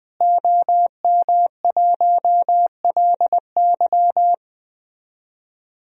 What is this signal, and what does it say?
Morse code, heard as a keyed tone, about 20 wpm, 700 Hz. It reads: OM1LY
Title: DXCC-100 Call Signs - 5 Characters - Repeated 3x - 20wpm